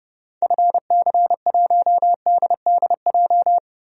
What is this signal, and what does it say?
Morse code, heard as a keyed tone, about 29 wpm, 700 Hz. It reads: FC1DDJ
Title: DXCC-100 Call Signs - 6 Characters - 30wpm